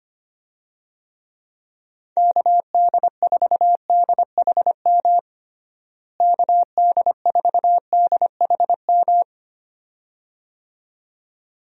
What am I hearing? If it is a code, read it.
Morse code, heard as a keyed tone, about 25 wpm, 700 Hz. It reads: KD4DHM KD4DHM
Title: USA Call Signs - 6 Characters - Repeated 2x - 25wpm